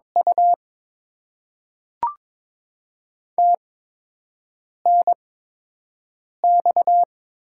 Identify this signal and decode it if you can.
Morse code, heard as a keyed tone, about 22 wpm, spaced as for 6 wpm, 700 Hz. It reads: U TNX
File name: Top 20 QSO Elements - 8x Character Spacing - 22wpm